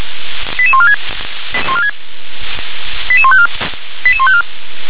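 Five-tone selective call (ZVEI), heard as quick runs of five tones, tones of about 70 ms. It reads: 88146 88154 88154
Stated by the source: ZVEI five-tone call 88786 received by SDR.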